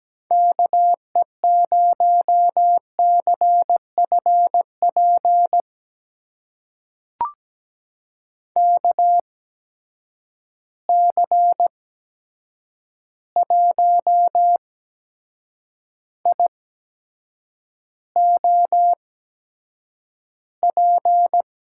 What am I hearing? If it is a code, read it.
Morse code, heard as a keyed tone, about 17 wpm, spaced as for 7 wpm, 700 Hz. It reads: KE0CFP KC1IOP
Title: USA Call Signs - 6 Characters - 8x Character Spacing - 17wpm